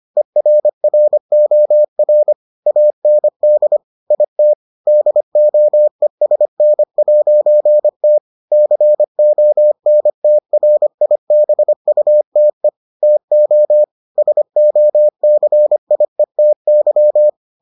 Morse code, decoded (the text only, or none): ERROR AND IT DOESN'T CONTRIBUTE TO SOCIETY